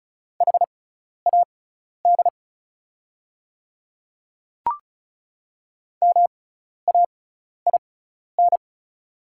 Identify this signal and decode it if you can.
Morse code, heard as a keyed tone, about 36 wpm, spaced as for 12 wpm, 690 Hz. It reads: HAD MAIN
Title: 6x Character Spacing - Top 300 Words - 35wpm